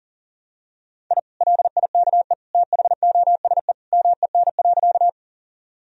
Morse code, heard as a keyed tone, about 40 wpm, 700 Hz. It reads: I LIKE THOSE MEN.
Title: Sentences from Top 1000 Words - Review Part 1 - 40wpm